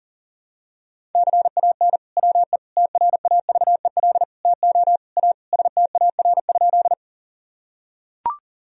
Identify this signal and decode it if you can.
Morse code, heard as a keyed tone, about 37 wpm, 700 Hz. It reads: CAN WE TRAVEL TO A STAR?
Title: Sentences from Top 1000 Words - Review Part 2 - 40wpm